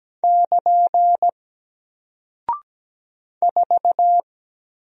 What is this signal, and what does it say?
Morse code, heard as a keyed tone, about 17 wpm, 700 Hz. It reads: <KN> 4
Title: Single Character (Letter:Number:Punctuation:Prosign) Alt 17wpm